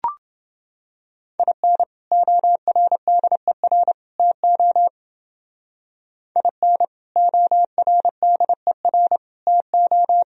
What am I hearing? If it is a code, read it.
Morse code, heard as a keyed tone, about 28 wpm, 700 Hz. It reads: IN ORDER TO IN ORDER TO